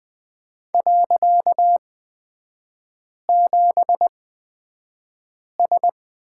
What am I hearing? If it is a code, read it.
Morse code, heard as a keyed tone, about 20 wpm, spaced as for 5 wpm, 690 Hz. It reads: .7S
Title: 255 Lesson 41e - Period - Three Characters 20wpm